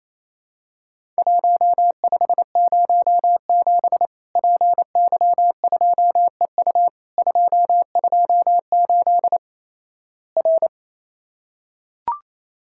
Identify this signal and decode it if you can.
Morse code, heard as a keyed tone, about 28 wpm, 700 Hz. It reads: 1507 PY2EU 228 R
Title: Worked All Europe DX Contest - QTC Exchange - 28wpm